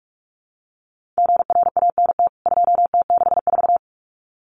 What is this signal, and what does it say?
Morse code, heard as a keyed tone, about 44 wpm, 700 Hz. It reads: GRANT 2T64
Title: CWT Exchange 45wpm